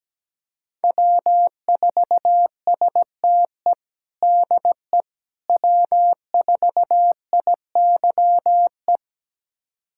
Morse code, heard as a keyed tone, about 17 wpm, 700 Hz. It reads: W4STE DE W4IYE
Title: USA Callsign DE Callsign - 5 Characters - 17wpm